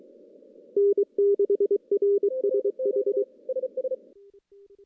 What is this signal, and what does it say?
Morse code, heard as a keyed tone, about 23 wpm, 400 Hz. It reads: N6RSH
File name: CW / CW-SS-k5tr- -1700